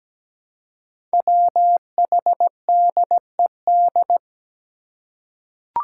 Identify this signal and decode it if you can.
Morse code, heard as a keyed tone, about 17 wpm, 700 Hz. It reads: WHDED
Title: Five Challenging Letter-Number Combinations 17wpm